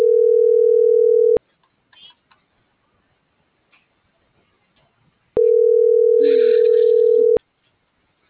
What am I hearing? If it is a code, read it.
Ringback tone (North America), 440 + 480 Hz.